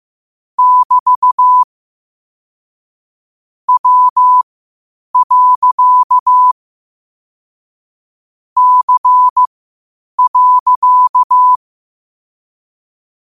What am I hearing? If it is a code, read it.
Morse code, heard as a keyed tone, about 15 wpm, spaced as for 8 wpm, 1000 Hz. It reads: <BT> W. C.
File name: qotdfiles / 8wpm